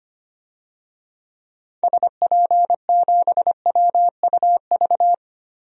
Morse code, heard as a keyed tone, about 25 wpm, 700 Hz. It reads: SP7WUV
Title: DXCC-100 Call Signs - 6 Characters - 25wpm